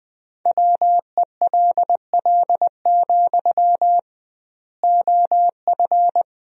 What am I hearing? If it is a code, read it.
Morse code, heard as a keyed tone, about 21 wpm, 700 Hz. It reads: WELL, OF
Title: 2x Word-Spacing - Sentences from Top 500 Words - 20wpm